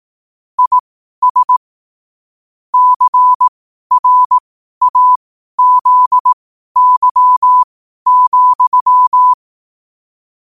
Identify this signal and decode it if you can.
Morse code, heard as a keyed tone, about 18 wpm, spaced as for 12 wpm, 1000 Hz. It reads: IS CRAZY,
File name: qotdfiles / 12wpm